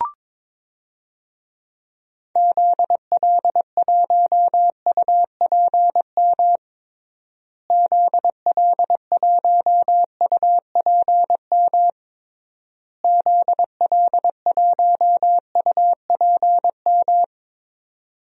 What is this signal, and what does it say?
Morse code, heard as a keyed tone, about 22 wpm, 700 Hz. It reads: ZL1UPM ZL1UPM ZL1UPM
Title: DXCC-100 Call Signs - 6 Characters - Repeated 3x - 22wpm